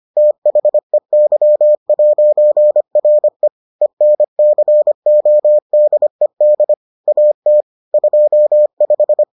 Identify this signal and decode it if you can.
Morse code, heard as a keyed tone, about 25 wpm, 600 Hz. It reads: THEY'RE ENCODED AT 25